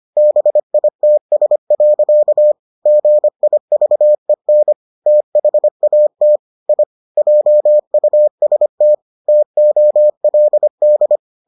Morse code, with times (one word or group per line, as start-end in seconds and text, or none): 0.2-2.5 BITS.
2.9-4.7 GIVEN
5.1-6.4 THAT
6.7-6.8 I
7.2-8.9 JUST
9.3-11.2 TOLD